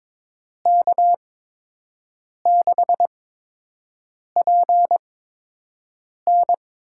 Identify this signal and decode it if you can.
Morse code, heard as a keyed tone, about 22 wpm, spaced as for 6 wpm, 700 Hz. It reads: K6PN
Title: USA Call Signs - 6 Characters - 8x Character Spacing - 22wpm